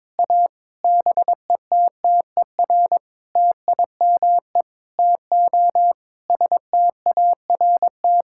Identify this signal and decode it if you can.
Morse code, heard as a keyed tone, about 22 wpm, 700 Hz. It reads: A BETTER TIME TO START
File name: Sentences with 7-Word Phrases - 22wpm